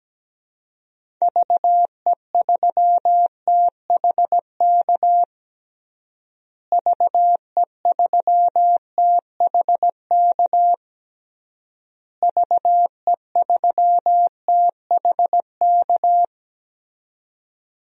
Morse code, heard as a keyed tone, about 17 wpm, 700 Hz. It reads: VE3THK VE3THK VE3THK
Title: DXCC-100 Call Signs - 6 Characters - Repeated 3x - 17wpm